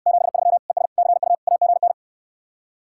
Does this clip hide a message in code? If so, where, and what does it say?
Yes: Morse code from 0.1 to 1.9 s, heard as a keyed tone, about 68 wpm, 700 Hz. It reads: 73 ES CU AGN